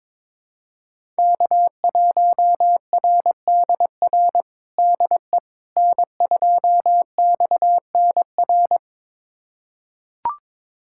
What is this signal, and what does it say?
Morse code, heard as a keyed tone, about 22 wpm, 700 Hz. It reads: K1RDR DE N2XNR
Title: USA Callsign DE Callsign - 5 Characters - 22wpm